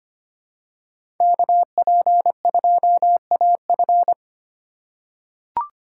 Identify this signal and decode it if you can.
Morse code, heard as a keyed tone, about 25 wpm, 700 Hz. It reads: KP2AF
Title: USA Call Signs - 5 Characters - 6x Character Spacing - 25wpm